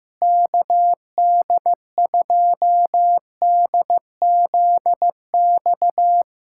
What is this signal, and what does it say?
Morse code, heard as a keyed tone, about 15 wpm, 700 Hz. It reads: KD2DZX
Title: USA Callsign DE Callsign - 6 Characters - 15wpm